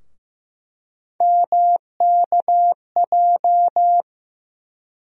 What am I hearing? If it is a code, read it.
Morse code, heard as a keyed tone, about 15 wpm, 700 Hz. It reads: MKJ
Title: USA Call Sign Postfixes - 3 Letters 15wpm